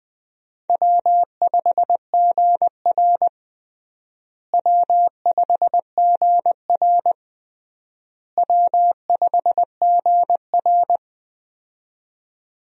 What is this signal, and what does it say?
Morse code, heard as a keyed tone, about 20 wpm, 700 Hz. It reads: W5GR W5GR W5GR
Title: USA Call Signs - 4 Characters - Repeated 3x - 20wpm